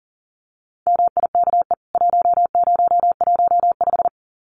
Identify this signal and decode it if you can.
Morse code, heard as a keyed tone, about 40 wpm, 700 Hz. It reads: MIKE 1015
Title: CWT Exchange 40wpm